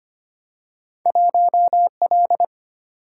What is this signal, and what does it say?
Morse code, heard as a keyed tone, about 25 wpm, 700 Hz. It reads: 1L
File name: Two Letter-Number Combinations - 25wpm